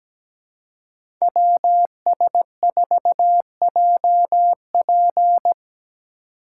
Morse code, heard as a keyed tone, about 17 wpm, 700 Hz. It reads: WS4JP
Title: USA Call Signs - 5 Characters - 2x Character Spacing - 17wpm